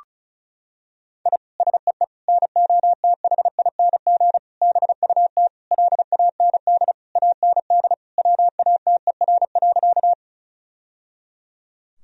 Morse code, heard as a keyed tone, about 36 wpm, 700 Hz. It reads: I SEE NOTHING BUT LAND AND WATER.
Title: Sentences from Top 1000 Words - Review Part 1 - 35wpm